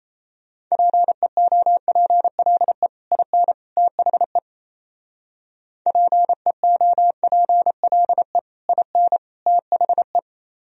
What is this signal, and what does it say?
Morse code, heard as a keyed tone, about 35 wpm, 700 Hz. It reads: PEOPLE IN THE PEOPLE IN THE